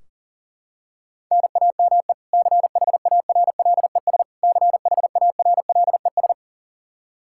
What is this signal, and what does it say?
Morse code, heard as a keyed tone, about 40 wpm, 700 Hz. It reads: NAME CHARLES CHARLES
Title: Practice QSO - Exchange 1 - Beginning - 40wpm